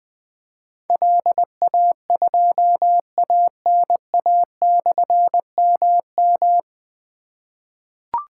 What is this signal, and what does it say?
Morse code, heard as a keyed tone, about 20 wpm, 700 Hz. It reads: LA2ANA/MM
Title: Difficult Call Signs 20wpm